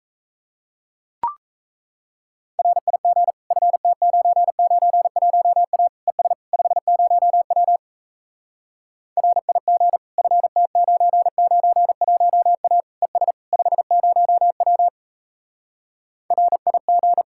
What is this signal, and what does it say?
Morse code, heard as a keyed tone, about 37 wpm, 700 Hz. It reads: RIG FT991A ES 50W RIG FT991A ES 50W RIG